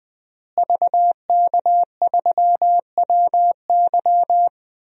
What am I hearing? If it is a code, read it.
Morse code, heard as a keyed tone, about 21 wpm, 700 Hz. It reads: VK3WY
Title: DXCC-100 Call Signs - 5 Characters - Repeated 2x - 20wpm